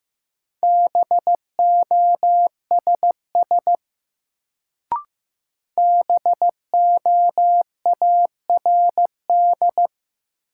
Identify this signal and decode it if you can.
Morse code, heard as a keyed tone, about 15 wpm, 700 Hz. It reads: BOSS BOARD
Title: Sentences from Top 700 Words - New Word Drill - 15wpm